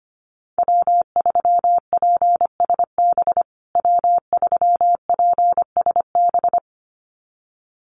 Morse code, heard as a keyed tone, about 25 wpm, 700 Hz. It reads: W3PSB W3PSB
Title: USA Call Signs Repeated 2x 25wpm